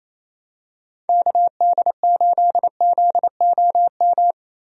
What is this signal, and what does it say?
Morse code, heard as a keyed tone, about 28 wpm, 700 Hz. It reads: KD8ZOM
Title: USA Call Signs - 6 Characters - 28wpm